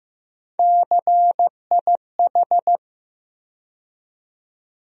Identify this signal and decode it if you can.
Morse code, heard as a keyed tone, about 15 wpm, 700 Hz. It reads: CIH
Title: USA Call Sign Postfixes - 3 Letters 15wpm